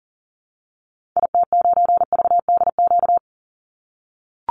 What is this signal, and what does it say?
Morse code, heard as a keyed tone, about 40 wpm, 700 Hz. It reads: IT9VDQ
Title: Call Signs - 40wpm